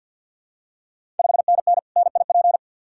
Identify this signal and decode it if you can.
Morse code, heard as a keyed tone, about 52 wpm, 700 Hz. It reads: 5NN NIP
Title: Ontario QSO Party Exchange 50wpm